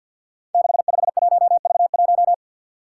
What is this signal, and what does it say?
Morse code, heard as a keyed tone, about 51 wpm, 700 Hz. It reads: 651V1
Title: Five Challenging Letter-Number Combinations 50wpm